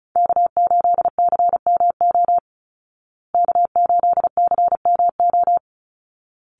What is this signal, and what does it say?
Morse code, heard as a keyed tone, about 34 wpm, 700 Hz. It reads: K8CMO K8CMO
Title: Call Signs Repeated 3x - 35wpm